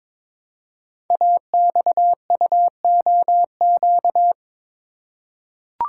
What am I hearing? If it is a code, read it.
Morse code, heard as a keyed tone, about 22 wpm, 700 Hz. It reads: AXUOQ
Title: Code Groups - 5 Letters - 22wpm